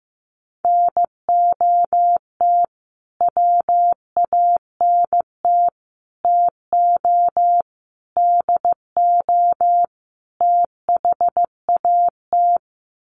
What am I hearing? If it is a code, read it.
Morse code, heard as a keyed tone, about 15 wpm, 700 Hz. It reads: NOT WANT TO DO THAT
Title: Sets of 5 Words Encore 15wpm